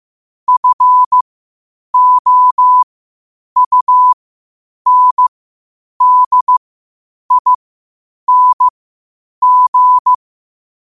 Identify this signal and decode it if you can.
Morse code, heard as a keyed tone, about 15 wpm, spaced as for 8 wpm, 1000 Hz. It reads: FOUNDING
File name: qotdfiles / 8wpm